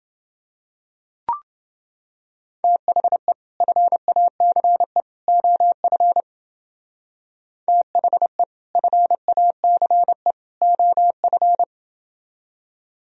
Morse code, heard as a keyed tone, about 29 wpm, 700 Hz. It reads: THE FACE OF THE FACE OF